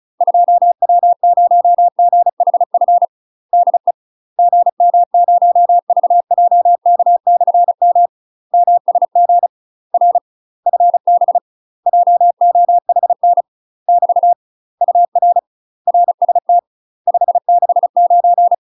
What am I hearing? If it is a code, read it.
Morse code, heard as a keyed tone, about 36 wpm, 700 Hz. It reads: 2W0GHF DE GM0VJK/M MSG R FB JOHN <BT> UR RST 569
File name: UK MASTER QSOs TRAINING / 35WPM / UK MASTER QSOs 0033